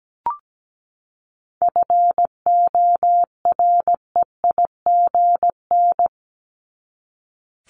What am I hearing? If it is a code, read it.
Morse code, heard as a keyed tone, about 17 wpm, 700 Hz. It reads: FOREIGN
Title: Top 1251-1500 Words 17wpm